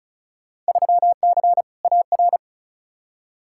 Morse code, heard as a keyed tone, about 36 wpm, 700 Hz. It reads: 3C AR
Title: Field Day Exchanges Encore - 35wpm